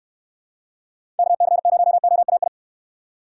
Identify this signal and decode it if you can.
Morse code, heard as a keyed tone, about 63 wpm, 700 Hz. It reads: KC0OAI